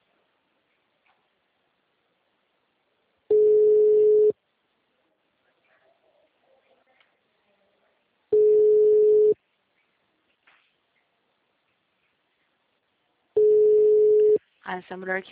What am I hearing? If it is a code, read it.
Ringback tone (Europe, 425 Hz), with speech.